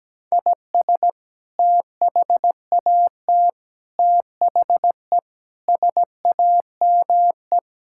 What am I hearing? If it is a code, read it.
Morse code, heard as a keyed tone, about 17 wpm, 700 Hz. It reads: IS THAT THE SAME
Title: Sentences from Top 1600 Words 17wpm